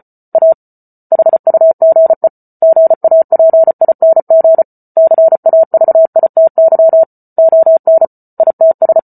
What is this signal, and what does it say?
Morse code, heard as a keyed tone, about 34 wpm, 650 Hz. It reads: A HUGE GAPING CAVITY ON ITS
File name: Book Two / 08-Chapter 8 - 34 WPM